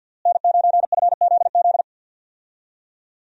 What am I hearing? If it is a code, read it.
Morse code, heard as a keyed tone, about 52 wpm, 700 Hz. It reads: N9FZZ
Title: USA Call Signs - 5 Characters - Repeated 3x - 50wpm